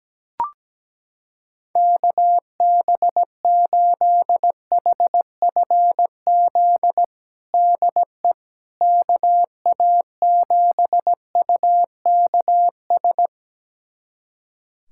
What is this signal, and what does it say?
Morse code, heard as a keyed tone, about 17 wpm, 700 Hz. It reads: KB8HFZ DE KA7UKS
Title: USA Callsign DE Callsign - 6 Characters - 17wpm